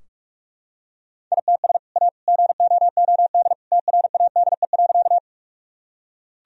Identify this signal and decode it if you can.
Morse code, heard as a keyed tone, about 45 wpm, 700 Hz. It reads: ITS A GOOD TRADE.
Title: Sentences from Top 1000 Words - Review Part 4 - 45wpm